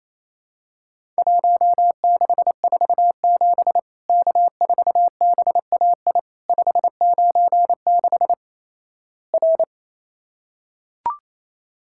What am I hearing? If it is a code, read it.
Morse code, heard as a keyed tone, about 28 wpm, 700 Hz. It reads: 1647 K4BAI 596 R
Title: Worked All Europe DX Contest - QTC Exchange - 28wpm